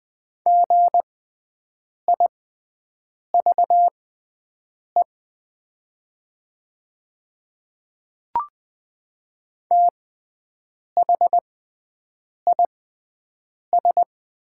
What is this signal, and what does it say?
Morse code, heard as a keyed tone, about 20 wpm, spaced as for 7 wpm, 700 Hz. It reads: GIVE THIS
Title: 6x Character Spacing - Top 200 Words - 20wpm